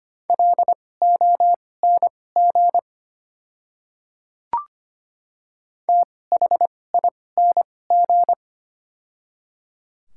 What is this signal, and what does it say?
Morse code, heard as a keyed tone, about 25 wpm, spaced as for 18 wpm, 700 Hz. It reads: LONG THING
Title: 2x Character Spacing - Top 200 Words - 25wpm